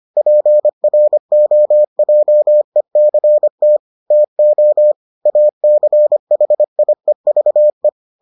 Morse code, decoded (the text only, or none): PROJECT TO ACHIEVE